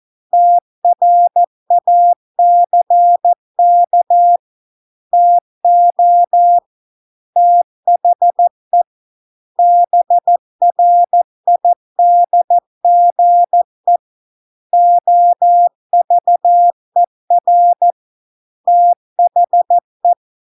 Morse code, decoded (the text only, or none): TRACK TO THE BRIDGE OVER THE